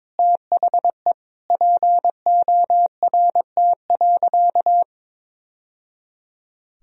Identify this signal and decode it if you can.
Morse code, heard as a keyed tone, about 22 wpm, 700 Hz. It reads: THE PORT.
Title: Sentences from Top 1000 Words - Review Part 1 - 22wpm